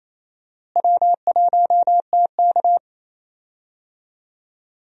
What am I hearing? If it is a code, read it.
Morse code, heard as a keyed tone, about 28 wpm, 700 Hz. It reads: W1TK
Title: USA Call Signs - 4 Characters - Repeated 2x - 28wpm